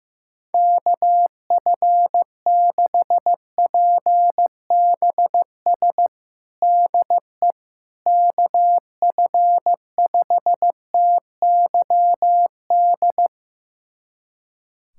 Morse code, heard as a keyed tone, about 15 wpm, 700 Hz. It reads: KF6PBS DE KF5TYD
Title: USA Callsign DE Callsign - 6 Characters - 15wpm